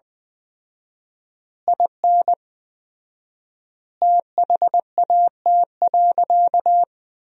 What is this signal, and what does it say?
Morse code, heard as a keyed tone, about 20 wpm, 700 Hz. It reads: IN THAT.
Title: 4x Word-Spacing - Sentences from Top 200 Words - 20wpm